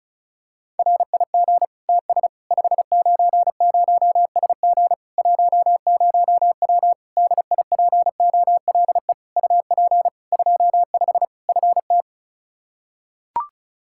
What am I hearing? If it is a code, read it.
Morse code, heard as a keyed tone, about 34 wpm, 700 Hz. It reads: RIG TS 590SG 10W DIPOLE UP 25 FT
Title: K1USN SST QSO Practice 35wpm